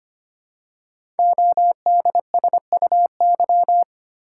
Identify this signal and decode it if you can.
Morse code, heard as a keyed tone, about 25 wpm, 700 Hz. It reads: ODSUY
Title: Code Groups - 5 Letters - 25wpm